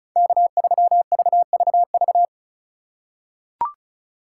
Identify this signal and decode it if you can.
Morse code, heard as a keyed tone, about 36 wpm, 700 Hz. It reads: K3VVV
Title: USA Call Signs - 5 Characters - 4x Character Spacing - 35wpm